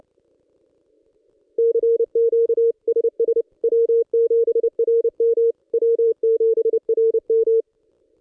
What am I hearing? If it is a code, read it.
Morse code, heard as a keyed tone, about 28 wpm, 460 Hz. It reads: CQ SS W7RM W7RM
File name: contests / ss-cw-k5tr-op-n5rz -1600